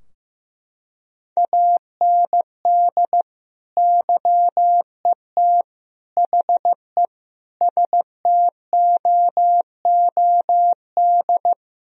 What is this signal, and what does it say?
Morse code, heard as a keyed tone, about 16 wpm, 700 Hz. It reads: AND YET HE STOOD
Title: Sentences from Top 1000 Words - Review Part 3 - 15wpm